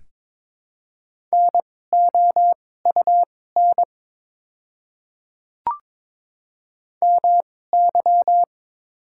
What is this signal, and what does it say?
Morse code, heard as a keyed tone, about 22 wpm, spaced as for 16 wpm, 700 Hz. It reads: NOUN MY
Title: 2x Character Spacing - Top 500 Words - 22wpm